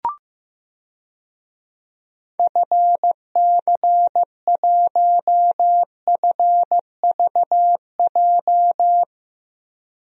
Morse code, heard as a keyed tone, about 15 wpm, 700 Hz. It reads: FC1FVJ
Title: DXCC-100 Call Signs - 6 Characters - Repeated 2x - 15wpm